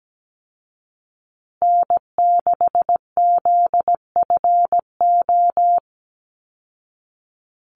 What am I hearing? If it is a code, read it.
Morse code, heard as a keyed tone, about 17 wpm, 700 Hz. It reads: N6ZFO
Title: Call Signs Repeated 2x - 17wpm